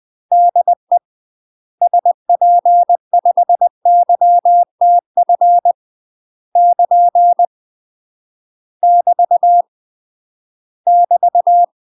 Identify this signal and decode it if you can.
Morse code, heard as a keyed tone, about 20 wpm, 700 Hz. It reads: DE SP5YTF <KN> <BT> <BT>
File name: UK MASTER QSOs TRAINING / 20WPM / UK MASTER QSOs 0069